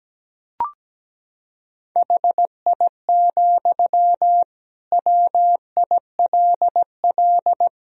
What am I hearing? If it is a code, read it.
Morse code, heard as a keyed tone, about 17 wpm, 700 Hz. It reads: HI, WILL
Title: Sentences from Top 1000 Words - Review Part 3 - 17wpm